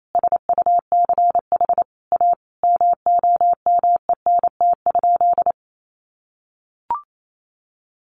Morse code, heard as a keyed tone, about 28 wpm, 700 Hz. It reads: SUCH A MOMENT?
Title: Sentences from Top 1000 Words 28wpm